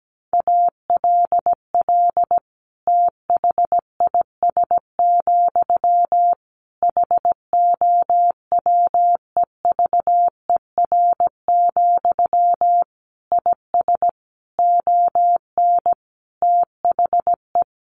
Morse code, heard as a keyed tone, about 17 wpm, 700 Hz. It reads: ALL THIS, HOWEVER, IS ON THE
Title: Sentences from Top 600 Words 17wpm